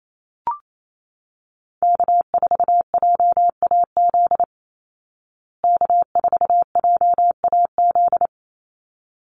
Morse code, heard as a keyed tone, about 28 wpm, 700 Hz. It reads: K4JAZ K4JAZ
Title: Call Signs Repeated 3x - 28wpm